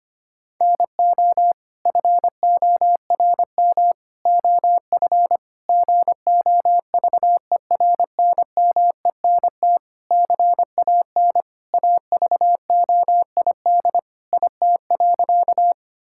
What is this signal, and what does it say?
Morse code, heard as a keyed tone, about 25 wpm, 700 Hz. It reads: NO FORM OF GOVERNMENT CAN AVOID IT.
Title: Sentences from Top 1000 Words - Review Part 6 - 25wpm